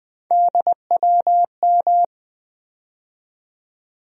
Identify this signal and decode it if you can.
Morse code, heard as a keyed tone, about 20 wpm, 700 Hz. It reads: DWM
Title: USA Call Sign Postfixes - 3 Letters 20wpm